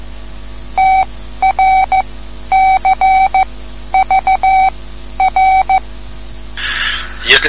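Morse code, heard as a keyed tone, about 15 wpm, spaced as for 11 wpm, 760 Hz. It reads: TRCVR
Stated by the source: Morse code lesson, third lesson in a series for learning the Morse alphabet.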